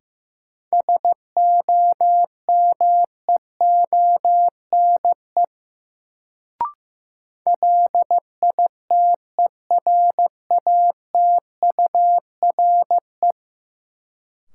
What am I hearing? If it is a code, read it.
Morse code, heard as a keyed tone, about 15 wpm, 700 Hz. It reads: SOMEONE LITERATURE
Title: Sentences from Top 600 Words - New Word Drill - 15wpm